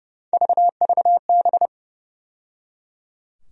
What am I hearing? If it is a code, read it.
Morse code, heard as a keyed tone, about 30 wpm, 700 Hz. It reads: VVB
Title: USA Call Sign Postfixes - 3 Letters 30wpm